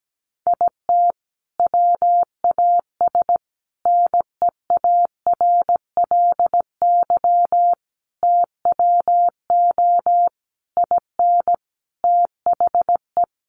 Morse code, heard as a keyed tone, about 17 wpm, 700 Hz. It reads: IT WAS NEARLY TWO IN THE
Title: Sentences from Top 1300 Words 17wpm